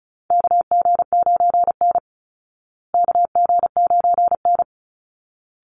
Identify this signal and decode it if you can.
Morse code, heard as a keyed tone, about 34 wpm, 700 Hz. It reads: KG9N KG9N
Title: Call Signs Repeated 3x - 35wpm